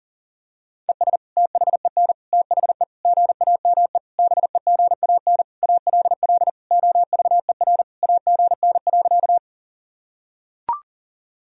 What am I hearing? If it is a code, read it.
Morse code, heard as a keyed tone, about 40 wpm, 700 Hz. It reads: ES THEN THE GAME BEGAN ALL OVER AGN.
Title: Sentences from Top 500 Words and Abbreviations 40wpm